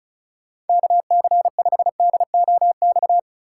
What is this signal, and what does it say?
Morse code, heard as a keyed tone, about 36 wpm, 700 Hz. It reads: KC5DOX
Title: USA Call Signs - 6 Characters - 4x Character Spacing - 35wpm